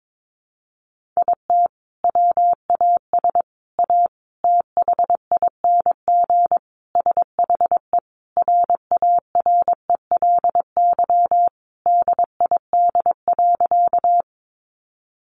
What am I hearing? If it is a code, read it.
Morse code, heard as a keyed tone, about 22 wpm, 700 Hz. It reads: IT WAS A THING SHE RARELY DID.
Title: Sentences from Top 1400 Words 22wpm